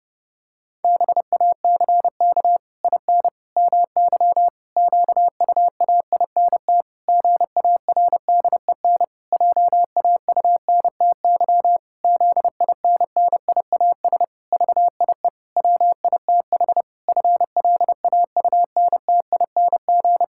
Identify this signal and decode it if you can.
Morse code, heard as a keyed tone, about 30 wpm, 700 Hz. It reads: BACK IN MY QUAINT GARDEN JAUNTY ZINNIAS VIE WITH FLAUNTING
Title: Pangram Challenge v2 - 30wpm